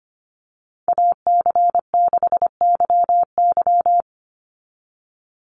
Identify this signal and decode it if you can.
Morse code, heard as a keyed tone, about 25 wpm, 700 Hz. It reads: AC6YY
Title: Call Signs Repeated 2x - 25wpm